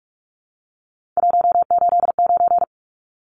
Morse code, heard as a keyed tone, about 44 wpm, 700 Hz. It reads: 189
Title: Numbers - 3-Digit 45wpm